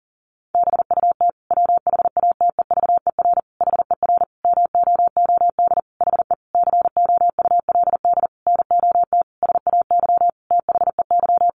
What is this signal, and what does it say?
Morse code, heard as a keyed tone, about 37 wpm, 700 Hz. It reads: BUT WHATEVER HER MOOD HE COULD NOT SAY THEY
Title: Sentences from Top 1100 Words 40wpm